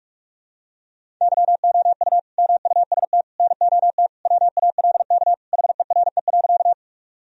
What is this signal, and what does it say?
Morse code, heard as a keyed tone, about 46 wpm, 700 Hz. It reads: YOU MUST NOT WALK HERE.
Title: Sentences from Top 1000 Words - Review Part 1 - 45wpm